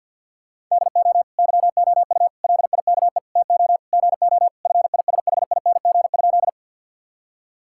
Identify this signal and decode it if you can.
Morse code, heard as a keyed tone, about 50 wpm, 700 Hz. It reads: DO YOU LIKE TO GO FISHING?
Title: Sentences from Top 1000 Words - Review Part 5 - 50wpm